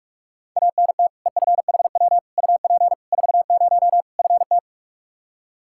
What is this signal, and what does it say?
Morse code, heard as a keyed tone, about 46 wpm, 700 Hz. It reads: ANT EFHW UP 40 FT
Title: Practice QSO - Exchange 2 - Info - 45wpm